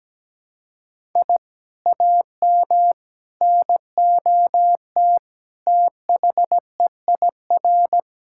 Morse code, heard as a keyed tone, about 17 wpm, 700 Hz. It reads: I AM NOT THEIR
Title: Sentences from Top 1000 Words - Review Part 6 - 17wpm